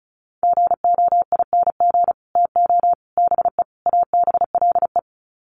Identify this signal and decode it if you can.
Morse code, heard as a keyed tone, about 34 wpm, 700 Hz. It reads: GOING TO BE ABLE
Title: Sets of 4 Words 35wpm